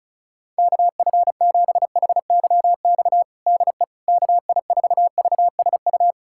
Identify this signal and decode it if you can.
Morse code, heard as a keyed tone, about 34 wpm, 700 Hz. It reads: KF7HYX DE KI4VSU
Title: USA Callsign DE Callsign - 6 Characters - 35wpm